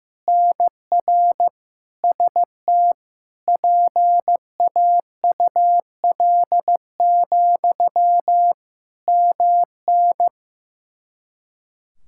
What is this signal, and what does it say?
Morse code, heard as a keyed tone, about 15 wpm, 700 Hz. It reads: NR ST PAUL, MN
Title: Practice QSO - Exchange 1 - Beginning - 15wpm